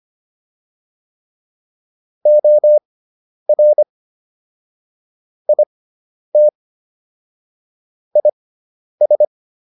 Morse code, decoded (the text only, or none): OR IT IS